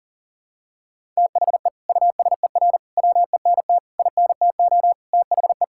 Morse code, heard as a keyed tone, about 40 wpm, 700 Hz. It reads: THE USER WENT INTO THE
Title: Sentences from Top 1000 Words - Review Part 5 - 40wpm